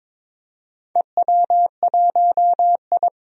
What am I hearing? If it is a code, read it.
Morse code, heard as a keyed tone, about 23 wpm, 700 Hz. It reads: EW1I
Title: DXCC-100 Call Signs - 4 Characters - 22wpm